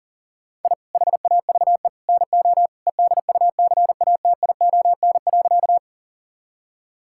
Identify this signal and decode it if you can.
Morse code, heard as a keyed tone, about 37 wpm, 700 Hz. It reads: I HAVE NO EDUCATION.
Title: Sentences from Top 1000 Words - Review Part 6 - 40wpm